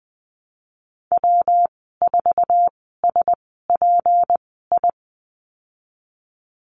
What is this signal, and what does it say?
Morse code, heard as a keyed tone, about 20 wpm, spaced as for 15 wpm, 700 Hz. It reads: W4SPI
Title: 154 Lesson 25g - Letter p - Call Signs 20wpm